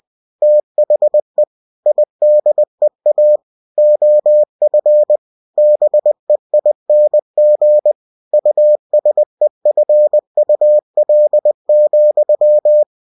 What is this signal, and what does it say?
Morse code, heard as a keyed tone, about 20 wpm, 600 Hz. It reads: THE IDEA OF BEING USEFUL,
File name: cw-books / winnie-the-pooh / winnie-20wpm-page-0062